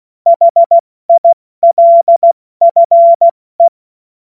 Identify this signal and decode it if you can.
Morse code, heard as a keyed tone, about 16 wpm, spaced as for 14 wpm, 680 Hz. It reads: HILFE
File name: Morsekurs / Witze / Witz 09. Februar 2021, 70 F80